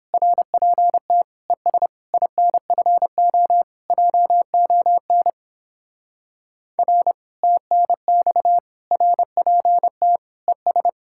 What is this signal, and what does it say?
Morse code, heard as a keyed tone, about 29 wpm, 700 Hz. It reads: RPT ES INFO JON R TNX RPT ES